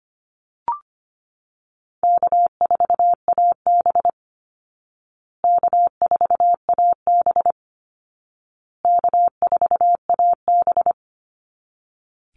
Morse code, heard as a keyed tone, about 25 wpm, 700 Hz. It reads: K4AB K4AB K4AB
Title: Easy Call Signs Repeated 3x - 25wpm